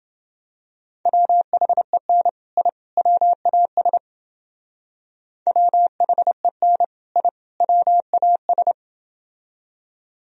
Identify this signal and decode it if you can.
Morse code, heard as a keyed tone, about 30 wpm, 700 Hz. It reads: WHEN I WAS WHEN I WAS